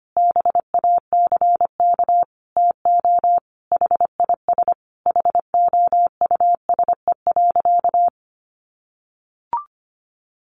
Morse code, heard as a keyed tone, about 25 wpm, 700 Hz. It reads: BACK TO HIS HOUSE.
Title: Additional Easy Phrases 25wpm